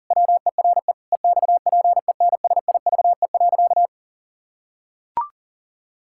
Morse code, heard as a keyed tone, about 40 wpm, 700 Hz. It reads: WERE EXPENSIVE.
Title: Pangram Challenge v2 - 40wpm